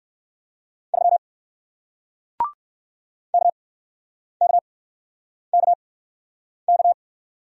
Morse code, heard as a keyed tone, about 60 wpm, spaced as for 9 wpm, 700 Hz. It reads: 3 KKKK